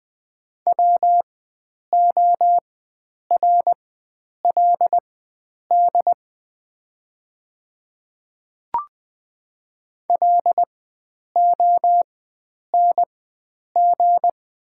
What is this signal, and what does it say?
Morse code, heard as a keyed tone, about 20 wpm, spaced as for 9 wpm, 690 Hz. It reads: WORLD LONG
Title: 4x Character Spacing - Top 300 Words - 20wpm